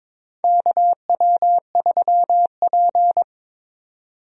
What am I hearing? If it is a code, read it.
Morse code, heard as a keyed tone, about 22 wpm, 700 Hz. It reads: KW3P
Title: USA Call Signs - 4 Characters - Repeated 2x - 22wpm